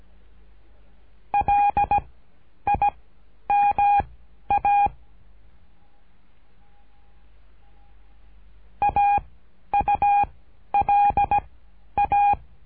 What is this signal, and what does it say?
Morse code, heard as a keyed tone, about 17 wpm, spaced as for 10 wpm, 810 Hz. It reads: LIMA AULA
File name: cursos / CURSO DE TELEGRAFIA / Telegrafia - Trilha 13